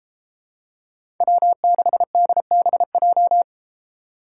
Morse code, heard as a keyed tone, about 31 wpm, 700 Hz. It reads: W6DBJ